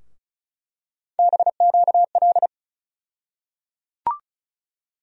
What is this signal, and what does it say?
Morse code, heard as a keyed tone, about 36 wpm, 700 Hz. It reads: BQL
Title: USA Call Sign Postfixes - 3 Letters 35wpm